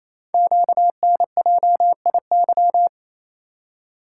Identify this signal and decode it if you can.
Morse code, heard as a keyed tone, about 29 wpm, 700 Hz. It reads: QNJIY
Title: Code Groups - 5 Letters - 28wpm